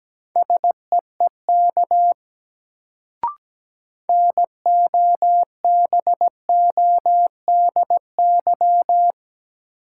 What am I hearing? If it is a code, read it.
Morse code, heard as a keyed tone, about 17 wpm, 700 Hz. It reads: SEEK NOBODY
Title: Sentences from Top 1600 Words - New Word Drill - 17wpm